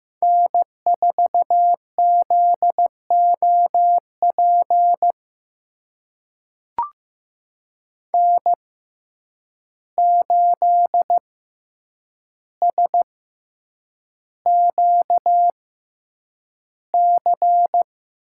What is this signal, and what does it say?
Morse code, heard as a keyed tone, about 15 wpm, spaced as for 8 wpm, 700 Hz. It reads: N4ZOP N8SQC
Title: USA Call Signs - 5 Characters - 6x Character Spacing - 15wpm